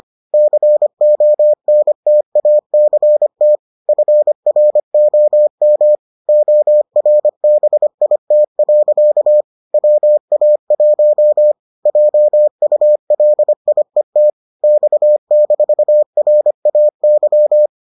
Morse code, heard as a keyed tone, about 25 wpm, 600 Hz. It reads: CONTACT FROM ORBIT. WA1 JULIET X-RAY